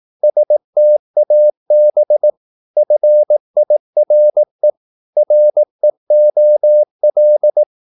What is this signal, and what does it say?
Morse code, heard as a keyed tone, about 18 wpm, 600 Hz. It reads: STAB FIRE REOL
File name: Korte ord DK 18wpm